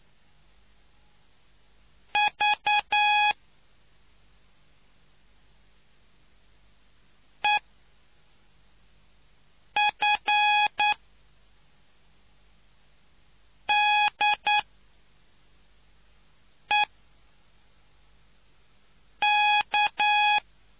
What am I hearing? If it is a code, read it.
Morse code, heard as a keyed tone, about 9 wpm, 840 Hz. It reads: V E F D E K